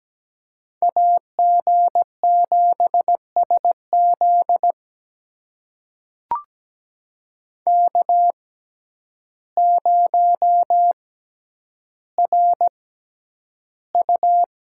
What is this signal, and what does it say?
Morse code, heard as a keyed tone, about 17 wpm, spaced as for 9 wpm, 700 Hz. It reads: AG7SZ K0RU
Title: USA Call Signs - 6x Character Spacing - 17wpm